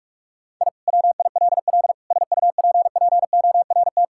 Morse code, heard as a keyed tone, about 46 wpm, 700 Hz. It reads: I WILL SUPPORT
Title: Sentences from Top 1000 Words - Review Part 5 - 45wpm